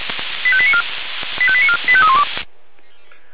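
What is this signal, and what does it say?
Five-tone selective call (ZVEI), heard as quick runs of five tones, tones of about 70 ms. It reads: 85894 85894 85312
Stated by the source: Received by SDR as ZVEI five-tone selective call.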